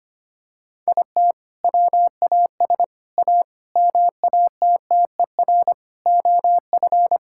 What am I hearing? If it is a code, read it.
Morse code, heard as a keyed tone, about 25 wpm, 700 Hz. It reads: IT WAS A MATTER OF
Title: Sentences from Top 1000 Words - Review Part 5 - 25wpm